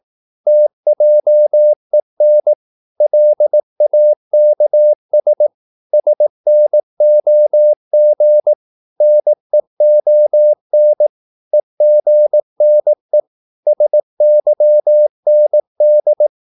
Morse code, heard as a keyed tone, about 18 wpm, 600 Hz. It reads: TJEN LAKS SNOG NEON EGNE SYND
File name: Korte ord DK 18wpm